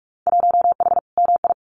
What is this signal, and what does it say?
Morse code, heard as a keyed tone, about 46 wpm, 700 Hz. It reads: 1H MI
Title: Winter Field Day Exchanges 45wpm